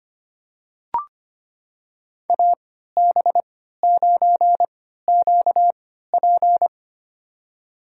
Morse code, heard as a keyed tone, about 25 wpm, spaced as for 14 wpm, 700 Hz. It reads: AB9QP
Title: USA Call Signs - 5 Characters - 3x Character Spacing - 25wpm